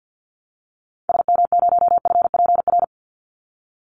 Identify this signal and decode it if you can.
Morse code, heard as a keyed tone, about 48 wpm, 700 Hz. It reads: SM0FPR
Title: Call Signs Repeated 2x - 50wpm